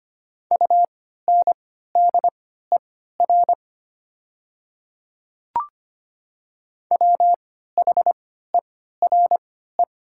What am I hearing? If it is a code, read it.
Morse code, heard as a keyed tone, about 25 wpm, spaced as for 14 wpm, 700 Hz. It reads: UNDER WHERE
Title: 3x Character Spacing - Top 200 Words - 25wpm